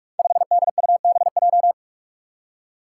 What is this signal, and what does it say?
Morse code, heard as a keyed tone, about 44 wpm, 700 Hz. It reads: 5DUBJ
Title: Five Challenging Letter-Number Combinations 45wpm